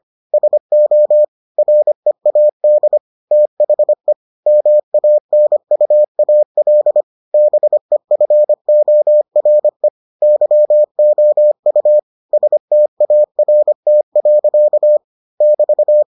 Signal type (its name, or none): Morse code